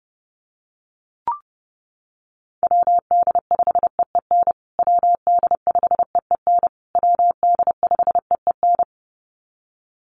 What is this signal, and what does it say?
Morse code, heard as a keyed tone, about 28 wpm, 700 Hz. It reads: WD5EEN WD5EEN WD5EEN
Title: USA Call Signs Repeated 3x 30wpm